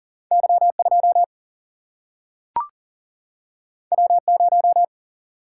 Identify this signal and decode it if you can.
Morse code, heard as a keyed tone, about 40 wpm, 700 Hz. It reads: Y2 W0
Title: Single-Letter Single-Number 40wpm